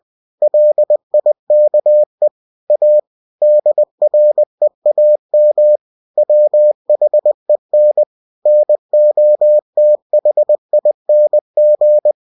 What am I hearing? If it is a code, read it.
Morse code, heard as a keyed tone, about 20 wpm, 600 Hz. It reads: LIKE A DREAM WHEN NOTHING
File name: cw-books / winnie-the-pooh / winnie-20wpm-page-0103